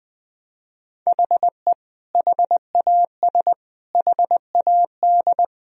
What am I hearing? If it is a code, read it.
Morse code, heard as a keyed tone, about 20 wpm, 700 Hz. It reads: HE HAS HAD
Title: Sentences from Top 1000 Words - Review Part 3 - 20wpm